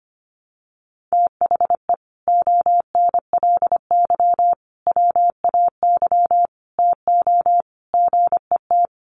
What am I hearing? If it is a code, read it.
Morse code, heard as a keyed tone, about 25 wpm, 700 Hz. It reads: THE ONLY WAY TO GET
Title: Sets of 5 Words Encore2 25wpm